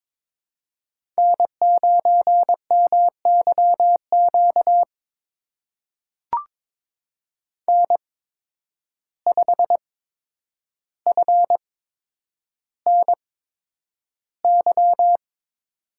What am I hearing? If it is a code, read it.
Morse code, heard as a keyed tone, about 22 wpm, spaced as for 9 wpm, 700 Hz. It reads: N9MYQ N5FNY
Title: USA Call Signs - 5 Characters - 8x Character Spacing - 22wpm